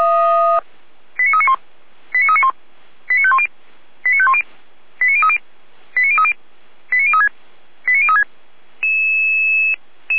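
Five-tone selective call (ZVEI), heard as quick runs of five tones, tones of about 70 ms. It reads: 89391 89391 89510 89510 89030 89030 89036 89036
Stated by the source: ZVEI five-tone call received on an SDR.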